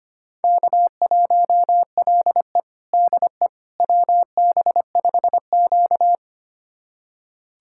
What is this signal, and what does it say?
Morse code, heard as a keyed tone, about 25 wpm, 700 Hz. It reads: K1LE DE WB5Q
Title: USA Callsign DE Callsign - 4 Characters - 25wpm